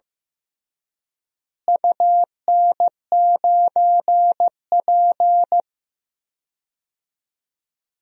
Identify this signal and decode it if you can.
Morse code, heard as a keyed tone, about 15 wpm, 700 Hz. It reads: UN9P
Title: DXCC-100 Call Signs - 4 Characters - Repeated 2x - 15wpm